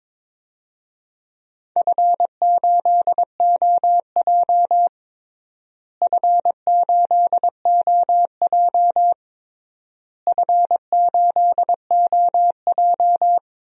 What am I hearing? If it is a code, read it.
Morse code, heard as a keyed tone, about 21 wpm, 700 Hz. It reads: F8OJ F8OJ F8OJ
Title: DXCC-100 Call Signs - 4 Characters - Repeated 3x - 22wpm